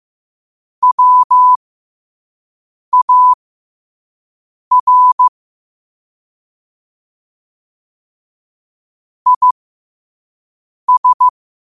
Morse code, heard as a keyed tone, about 15 wpm, spaced as for 5 wpm, 1000 Hz. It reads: WAR IS
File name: qotdfiles / 5wpm